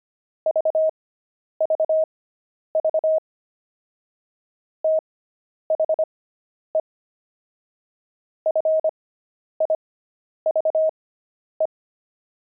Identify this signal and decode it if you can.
Morse code, heard as a keyed tone, about 25 wpm, spaced as for 10 wpm, 630 Hz. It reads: VVV THE FIVE
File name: pangram-fs-10-cs-25-jumping